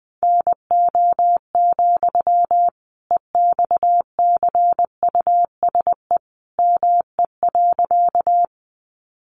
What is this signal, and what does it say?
Morse code, heard as a keyed tone, about 20 wpm, 700 Hz. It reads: NO, EXCUSE ME.
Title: Sentences from Top 700 Words 20wpm